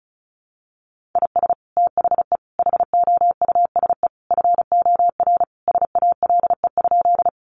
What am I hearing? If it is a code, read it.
Morse code, heard as a keyed tone, about 34 wpm, 700 Hz. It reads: IS THE HOUSE FOR SALE?
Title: Sentences from Top 1400 Words 35wpm